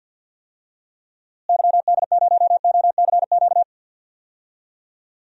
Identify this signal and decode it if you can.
Morse code, heard as a keyed tone, about 51 wpm, 700 Hz. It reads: YD0OCQ
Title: DXCC-100 Call Signs - 6 Characters - Repeated 3x - 50wpm